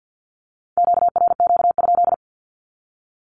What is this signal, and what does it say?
Morse code, heard as a keyed tone, about 50 wpm, 700 Hz. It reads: QRQ?
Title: Q-Codes with Spoken Meaning - 50wpm